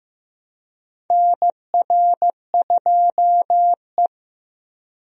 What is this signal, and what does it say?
Morse code, heard as a keyed tone, about 16 wpm, 700 Hz. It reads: NR2E
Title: USA Call Signs - 4 Characters - Repeated 3x - 15wpm